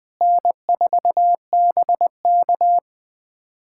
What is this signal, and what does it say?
Morse code, heard as a keyed tone, about 20 wpm, 700 Hz. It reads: N4BK
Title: USA Callsign DE Callsign - 4 Characters - 20wpm